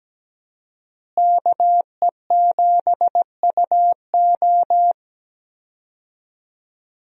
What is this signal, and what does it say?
Morse code, heard as a keyed tone, about 17 wpm, 700 Hz. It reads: KE7UO
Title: USA Call Signs - 5 Characters - Repeated 3x - 17wpm